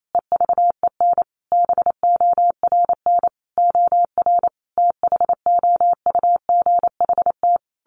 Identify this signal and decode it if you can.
Morse code, heard as a keyed tone, about 28 wpm, 700 Hz. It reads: EVEN BORN OR THOUGHT
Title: A Sound Of Thunder 28wpm